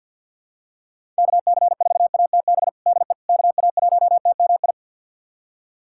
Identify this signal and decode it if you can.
Morse code, heard as a keyed tone, about 49 wpm, 700 Hz. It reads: KC4ATB DE KA1TMI
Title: USA Callsign DE Callsign - 6 Characters - 50wpm